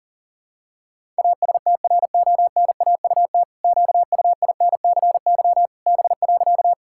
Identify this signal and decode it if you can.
Morse code, heard as a keyed tone, about 40 wpm, 700 Hz. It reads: ASTRONAUT QUINCY B.
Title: Pangram Challenge v2 - 40wpm